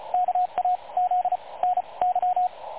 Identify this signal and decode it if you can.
Morse code, heard as a keyed tone, about 33 wpm, spaced as for 26 wpm, 730 Hz, with radio noise. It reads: KAZ NY
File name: dlazaawansowanych / nagraniazpasma / websdr recording 3565.0kHz